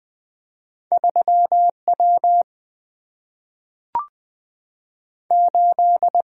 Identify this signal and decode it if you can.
Morse code, heard as a keyed tone, about 20 wpm, 700 Hz. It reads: 3 W 8
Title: Single-Letter Single-Number 20wpm